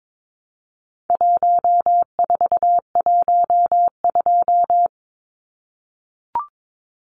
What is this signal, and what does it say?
Morse code, heard as a keyed tone, about 22 wpm, 700 Hz. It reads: 1412
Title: Numbers - 4-Digit 22wpm